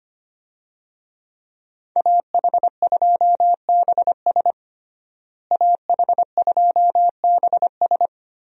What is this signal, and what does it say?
Morse code, heard as a keyed tone, about 25 wpm, 700 Hz. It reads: AH2BS AH2BS
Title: USA Call Signs - 5 Characters - Repeated 3x - 25wpm